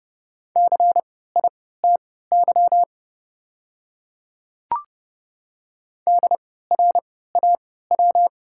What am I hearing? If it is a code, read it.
Morse code, heard as a keyed tone, about 30 wpm, spaced as for 17 wpm, 690 Hz. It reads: CITY DRAW
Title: 3x Character Spacing - Top 500 Words - 30wpm